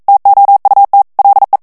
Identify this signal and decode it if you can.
Morse code, heard as a keyed tone, about 42 wpm, 800 Hz. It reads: TOUT LE
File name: CW / FICHIERS-MP3 / FRANCAIS TEXTE CLAIR 2 VITESSE 40 MOTS MINUTES